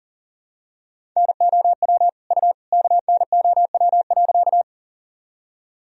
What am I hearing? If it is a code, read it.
Morse code, heard as a keyed tone, about 40 wpm, 700 Hz. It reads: NOW U KNOW.
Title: Sentences from Top 100 Words and Abbreviations 40wpm